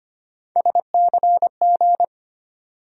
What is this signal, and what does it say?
Morse code, heard as a keyed tone, about 25 wpm, 700 Hz. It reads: SCG
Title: USA Call Sign Postfixes - 3 Letters 25wpm